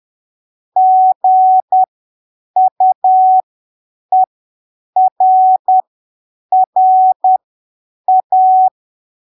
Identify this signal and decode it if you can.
Morse code, heard as a keyed tone, about 10 wpm, spaced as for 7 wpm, 750 Hz. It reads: GUERRA
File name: arquivos / texto / texto 14 10ppm